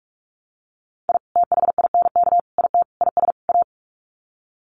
Morse code, heard as a keyed tone, about 44 wpm, 700 Hz. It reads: I THINK IT IS A
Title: Sets of 5 Words Encore 45wpm